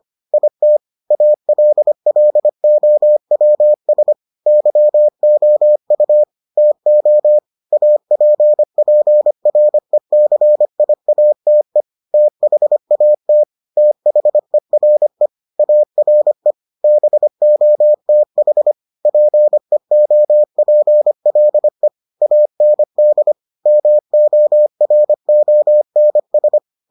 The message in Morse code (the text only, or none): IT ALLOWS YOU TO APPRECIATE THAT THERE ARE BOTH PEOPLE AND MORONS